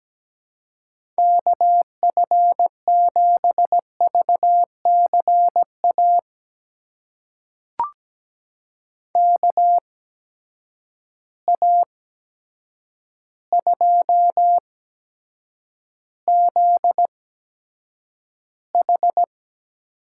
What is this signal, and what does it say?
Morse code, heard as a keyed tone, about 17 wpm, spaced as for 8 wpm, 700 Hz. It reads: KF7VCA KA2ZH
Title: USA Call Signs - 6 Characters - 8x Character Spacing - 17wpm